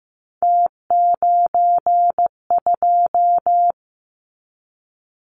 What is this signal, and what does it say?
Morse code, heard as a keyed tone, about 15 wpm, 700 Hz. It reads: T92
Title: Callsign Prefixes for Top 50 DXCC - 15wpm